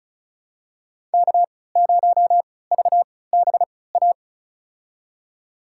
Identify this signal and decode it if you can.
Morse code, heard as a keyed tone, about 34 wpm, spaced as for 20 wpm, 700 Hz. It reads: K0VBA
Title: USA Call Signs - 5 Characters - 3x Character Spacing - 35wpm